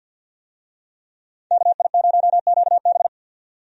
Morse code, heard as a keyed tone, about 51 wpm, 700 Hz. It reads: KI0QB
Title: USA Call Signs - 5 Characters - 50wpm